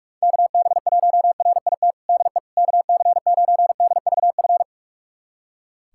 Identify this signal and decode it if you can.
Morse code, heard as a keyed tone, about 46 wpm, 700 Hz. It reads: KB1RIT DE KC9DVF
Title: USA Callsign DE Callsign - 6 Characters - 45wpm